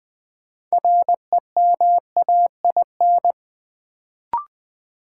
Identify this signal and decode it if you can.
Morse code, heard as a keyed tone, about 20 wpm, 700 Hz. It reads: REMAIN
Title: Sentences from Top 1200 Words - New Word Drill Alt - 20wpm